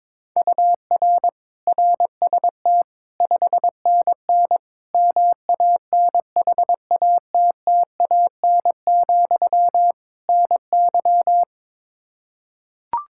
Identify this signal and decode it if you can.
Morse code, heard as a keyed tone, about 22 wpm, 700 Hz. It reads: UR RST 5NN MANHATTAN, NY
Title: Practice QSO - Exchange 1 - Beginning - 22wpm